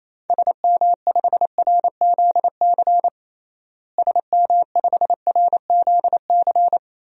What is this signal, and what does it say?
Morse code, heard as a keyed tone, about 28 wpm, 700 Hz. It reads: SM5RZC SM5RZC
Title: DXCC-100 Call Signs - 6 Characters - Repeated 2x - 28wpm